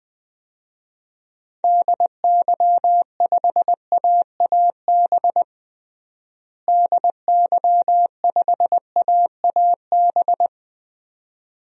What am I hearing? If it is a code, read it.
Morse code, heard as a keyed tone, about 20 wpm, 700 Hz. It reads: DY5AAB DY5AAB
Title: DXCC-100 Call Signs - 6 Characters - Repeated 2x - 20wpm